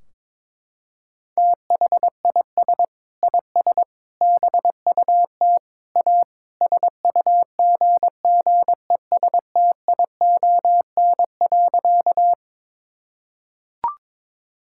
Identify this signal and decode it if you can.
Morse code, heard as a keyed tone, about 22 wpm, 700 Hz. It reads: THIS IS BUT A SUGGESTION.
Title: Sentences from Top 1600 Words 22wpm